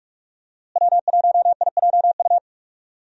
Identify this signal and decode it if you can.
Morse code, heard as a keyed tone, about 42 wpm, 700 Hz. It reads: W1IJU
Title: USA Call Signs - 5 Characters - Repeated 3x - 45wpm